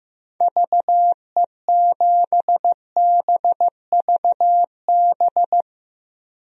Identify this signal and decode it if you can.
Morse code, heard as a keyed tone, about 15 wpm, 700 Hz. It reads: VE7BVB
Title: DXCC-100 Call Signs - 6 Characters - 15wpm